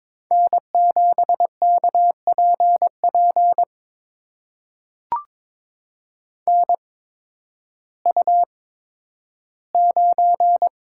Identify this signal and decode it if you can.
Morse code, heard as a keyed tone, about 22 wpm, spaced as for 12 wpm, 700 Hz. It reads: N7KPP NU9
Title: USA Call Signs - 8x Character Spacing - 22wpm